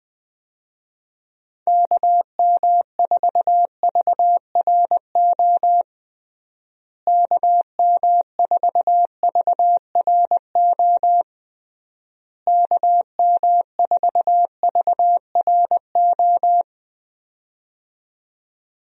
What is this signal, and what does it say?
Morse code, heard as a keyed tone, about 20 wpm, 700 Hz. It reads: KM4VRO KM4VRO KM4VRO
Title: USA Call Signs - 6 Characters - Repeated 3x - 20wpm